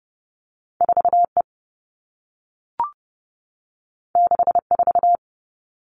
Morse code, heard as a keyed tone, about 30 wpm, 700 Hz. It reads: 4E 64
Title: All Cut Numbers - 2-Digit 30wpm